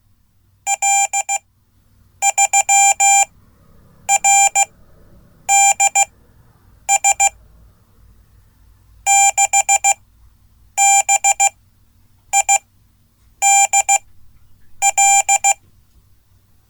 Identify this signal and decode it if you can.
Morse code, heard as a keyed tone, about 16 wpm, spaced as for 8 wpm, 790 Hz. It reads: L3RDS 6BIDL
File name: Code / 5min-a